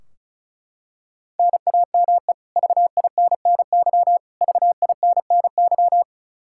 Morse code, heard as a keyed tone, about 34 wpm, 700 Hz. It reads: NAME VINNY VINNY
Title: K1USN SST QSO Practice 35wpm